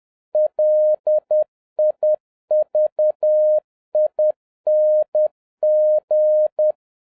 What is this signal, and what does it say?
Morse code, heard as a keyed tone, about 10 wpm, 610 Hz. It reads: LIVING
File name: 2011-07-mp3 / n41 EXCITING QRP CW OPERATION 10wpm